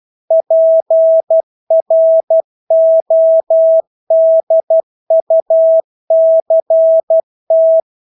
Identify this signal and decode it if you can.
Morse code, heard as a keyed tone, about 12 wpm, 650 Hz. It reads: PRODUCT
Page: Engelske ord 12 wpm